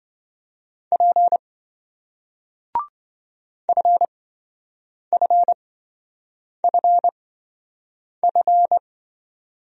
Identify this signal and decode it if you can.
Morse code, heard as a keyed tone, about 28 wpm, spaced as for 7 wpm, 700 Hz. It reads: P FFFF